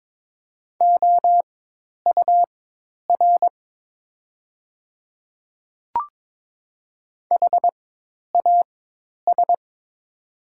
Morse code, heard as a keyed tone, about 22 wpm, spaced as for 10 wpm, 700 Hz. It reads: OUR HAS
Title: 4x Character Spacing - Top 300 Words - 22wpm